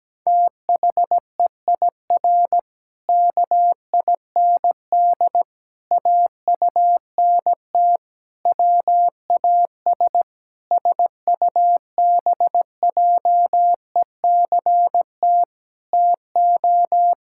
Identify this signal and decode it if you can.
Morse code, heard as a keyed tone, about 17 wpm, 700 Hz. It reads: THEIR KIND AUNT WAS SUBJECT TO
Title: Pangram Challenge v2 - 17wpm